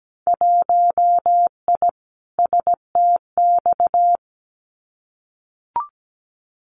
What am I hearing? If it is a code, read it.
Morse code, heard as a keyed tone, about 17 wpm, 700 Hz. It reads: 1I STX
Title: Winter Field Day Exchanges 17wpm